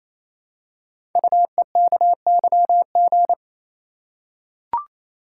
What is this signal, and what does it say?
Morse code, heard as a keyed tone, about 27 wpm, 700 Hz. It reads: UEKYG
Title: Code Groups - 5 Letters - 28wpm